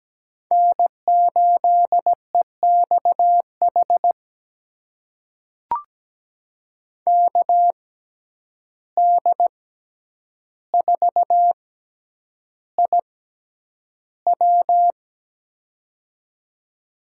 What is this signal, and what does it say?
Morse code, heard as a keyed tone, about 17 wpm, spaced as for 9 wpm, 700 Hz. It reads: N8EXH KD4IW
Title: USA Call Signs - 5 Characters - 6x Character Spacing - 17wpm